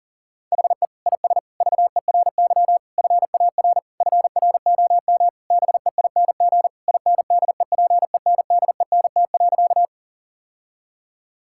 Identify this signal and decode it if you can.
Morse code, heard as a keyed tone, about 40 wpm, 700 Hz. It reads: HE IS VERY FAR FROM BEING INDEPENDENT.
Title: Sentences from Top 1000 Words - Review Part 6 - 40wpm